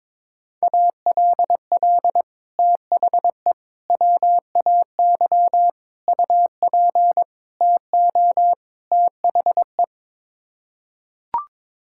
Morse code, heard as a keyed tone, about 22 wpm, 700 Hz. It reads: ALL THE WAY UP TO THE
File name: practice / 22wpm / Sets of 6 Words - 22wpm